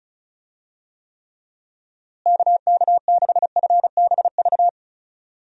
Morse code, heard as a keyed tone, about 36 wpm, 700 Hz. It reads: KK6FBV
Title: USA Call Signs - 6 Characters - Repeated 3x - 35wpm